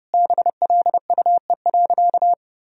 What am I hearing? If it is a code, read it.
Morse code, heard as a keyed tone, about 32 wpm, 700 Hz. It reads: BLUE.
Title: Sentences from Top 1000 Words - Review Part 3 - 30wpm